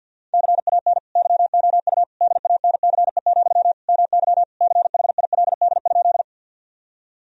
Morse code, heard as a keyed tone, about 50 wpm, 700 Hz. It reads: CAN YOU DANCE, MY CHILD?
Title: Sentences from Top 1000 Words - Review Part 4 - 50wpm